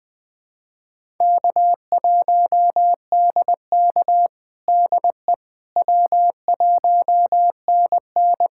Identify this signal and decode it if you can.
Morse code, heard as a keyed tone, about 20 wpm, 700 Hz. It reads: K1DK DE W1NN
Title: USA Callsign DE Callsign - 4 Characters - 20wpm